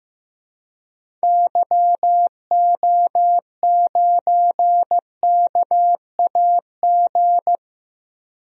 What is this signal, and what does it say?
Morse code, heard as a keyed tone, about 15 wpm, 700 Hz. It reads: YO9KAG
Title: DXCC-100 Call Signs - 6 Characters - Repeated 3x - 15wpm